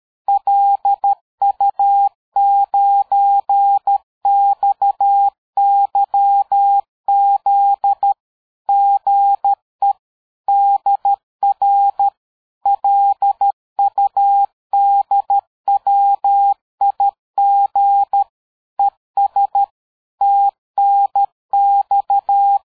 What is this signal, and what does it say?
Morse code, heard as a keyed tone, about 13 wpm, 800 Hz. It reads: LU9XYZ GE DR LUDWIG ES TNX
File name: dk5ke / QSO 60BpM